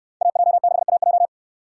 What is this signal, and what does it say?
Morse code, heard as a keyed tone, about 65 wpm, 700 Hz. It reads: AJ6AJ